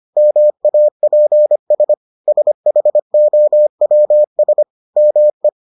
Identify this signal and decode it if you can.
Morse code, heard as a keyed tone, about 25 wpm, 600 Hz. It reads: MAPS SHOWS ME